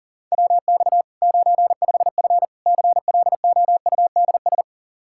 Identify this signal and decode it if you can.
Morse code, heard as a keyed tone, about 37 wpm, 700 Hz. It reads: WX 95F CLOUDS
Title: K1USN SST QSO Practice 40wpm